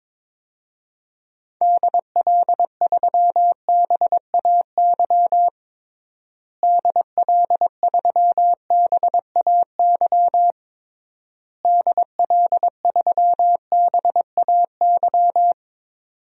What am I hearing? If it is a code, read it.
Morse code, heard as a keyed tone, about 22 wpm, 700 Hz. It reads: DL3BAY DL3BAY DL3BAY
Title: DXCC-100 Call Signs - 6 Characters - Repeated 3x - 22wpm